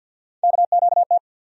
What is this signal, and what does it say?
Morse code, heard as a keyed tone, about 51 wpm, 700 Hz. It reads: KQT
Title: USA Call Sign Postfixes - 3 Letters 50wpm